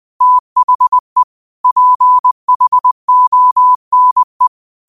Morse code, heard as a keyed tone, about 20 wpm, 1000 Hz. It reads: THE PHONE
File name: qotdfiles / 20wpm